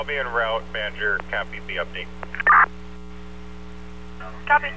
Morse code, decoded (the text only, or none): none